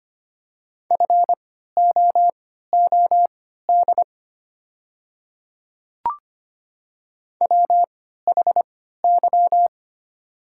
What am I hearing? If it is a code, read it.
Morse code, heard as a keyed tone, about 25 wpm, spaced as for 14 wpm, 700 Hz. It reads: FOOD WHY
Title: 3x Character Spacing - Top 500 Words - 25wpm